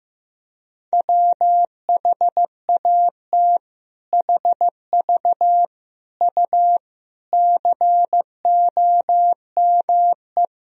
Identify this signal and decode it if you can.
Morse code, heard as a keyed tone, about 16 wpm, 700 Hz. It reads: WHAT HV U COME
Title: Sentences from Top 100 Words and Abbreviations 15wpm